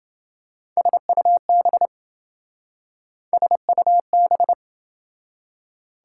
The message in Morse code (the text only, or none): SUB SUB